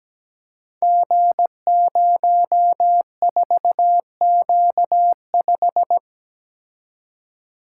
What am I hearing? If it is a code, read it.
Morse code, heard as a keyed tone, about 17 wpm, 700 Hz. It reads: G04Q5
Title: Code Groups - 5 Letter:Numbers - 17wpm